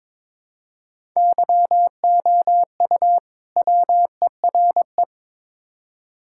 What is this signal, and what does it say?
Morse code, heard as a keyed tone, about 22 wpm, 700 Hz. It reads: YOU WERE
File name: Build A Phrase - 2-Word Phrases - 22wpm